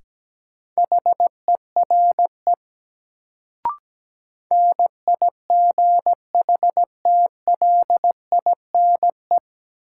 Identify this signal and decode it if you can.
Morse code, heard as a keyed tone, about 17 wpm, 700 Hz. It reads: HERE NIGHTLINE
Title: Sets of 5 Words - New Word Drill - 17wpm